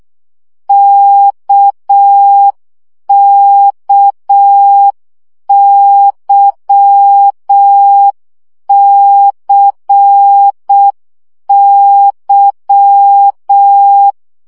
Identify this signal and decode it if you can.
Morse code, heard as a keyed tone, about 6 wpm, 800 Hz. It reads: KKYCY
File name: CW gyakorlas k c y 5 baud